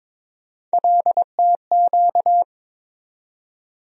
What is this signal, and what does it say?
Morse code, heard as a keyed tone, about 22 wpm, 700 Hz. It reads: LTQ
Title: USA Call Sign Postfixes - 3 Letters 22wpm